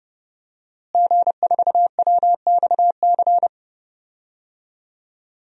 Morse code, heard as a keyed tone, about 30 wpm, 700 Hz. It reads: G4WXC
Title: DXCC-100 Call Signs - 5 Characters - Repeated 2x - 30wpm